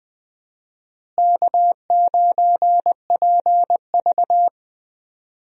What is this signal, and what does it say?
Morse code, heard as a keyed tone, about 20 wpm, 700 Hz. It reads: K9PV
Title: USA Call Signs - 4 Characters - Repeated 2x - 20wpm